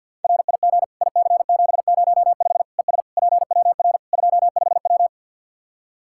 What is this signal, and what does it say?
Morse code, heard as a keyed tone, about 51 wpm, 700 Hz. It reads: RIG IC705 ES PWR 25W
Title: Practice QSO - Exchange 2 - Info - 50wpm